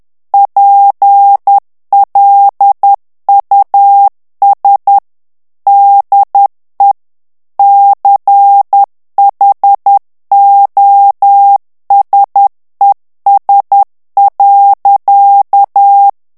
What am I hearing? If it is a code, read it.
Morse code, heard as a keyed tone, about 10 wpm, 800 Hz. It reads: PLUS DE CHOSES.
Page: Français - texte en clair 1 - vitesse : 10 mots minute